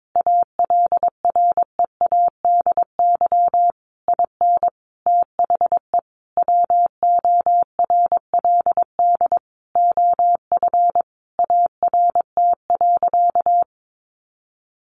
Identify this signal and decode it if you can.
Morse code, heard as a keyed tone, about 22 wpm, 700 Hz. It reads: ALREADY IN THE WORLD OF ART.
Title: Sentences from Top 1500 Words 22wpm